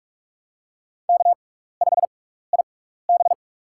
Morse code, heard as a keyed tone, about 46 wpm, spaced as for 15 wpm, 700 Hz. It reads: K5IB
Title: USA Call Signs - 4 Characters - 6x Character Spacing - 45wpm